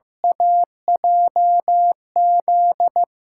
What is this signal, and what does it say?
Morse code, heard as a keyed tone, about 15 wpm, 700 Hz. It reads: AJZ
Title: USA Call Signs - 6 Characters - 8x Character Spacing - 15wpm